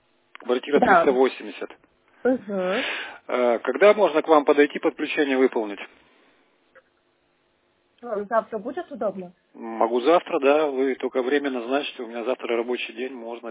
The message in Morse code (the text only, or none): none